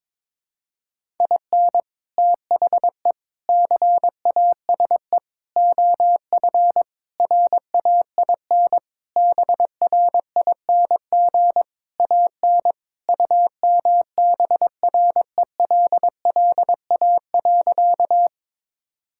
Morse code, heard as a keyed tone, about 22 wpm, 700 Hz. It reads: IN THE CASE OF RAIN BRING AN UMBRELLA.
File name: practice / 22wpm / Sentences with 3-Word Phrases - 22wpm